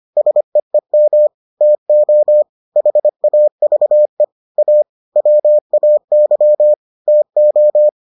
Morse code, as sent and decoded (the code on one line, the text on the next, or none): ... . -- / - --- / .... .- ...- . / .- / .-- .- -.-- / - ---
SEEM TO HAVE A WAY TO